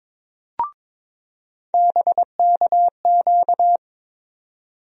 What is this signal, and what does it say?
Morse code, heard as a keyed tone, about 22 wpm, 700 Hz. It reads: BKQ
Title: USA Call Sign Postfixes - 3 Letters 22wpm